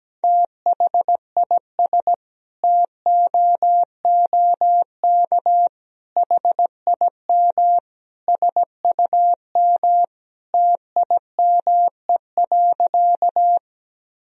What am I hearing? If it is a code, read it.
Morse code, heard as a keyed tone, about 17 wpm, 700 Hz. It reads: THIS TOOK HIM SUM TIME.
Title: Sentences from Top 300 Words and Abbreviations 17wpm